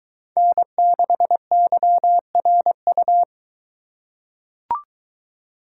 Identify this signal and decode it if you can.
Morse code, heard as a keyed tone, about 23 wpm, 700 Hz. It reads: N6YRU